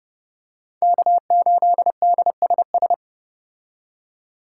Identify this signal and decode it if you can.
Morse code, heard as a keyed tone, about 30 wpm, 700 Hz. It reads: K8DSS
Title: K1USN SST QSO Practice - 30wpm